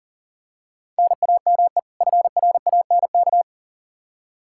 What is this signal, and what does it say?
Morse code, heard as a keyed tone, about 40 wpm, 700 Hz. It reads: NAME FRANK
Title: Practice QSO - Exchange 1 - Beginning - 40wpm